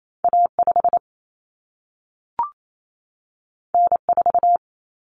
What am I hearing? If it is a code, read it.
Morse code, heard as a keyed tone, about 28 wpm, 700 Hz. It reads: A5 N4
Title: Common Cut Numbers - 2-Digit 28wpm